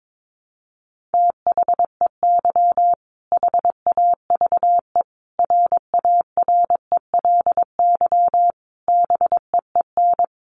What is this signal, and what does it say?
Morse code, heard as a keyed tone, about 22 wpm, 700 Hz. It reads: THEY HAVE RARELY BEEN
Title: Sentences from Top 1400 Words 22wpm